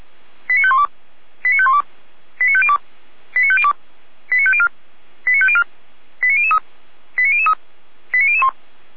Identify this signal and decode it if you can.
Five-tone selective call (ZVEI), heard as quick runs of five tones, tones of about 70 ms. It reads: 89512 89512 89502 89502 89504 89504 89003 89003 89001
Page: /455/SDR/ZVEI/89060/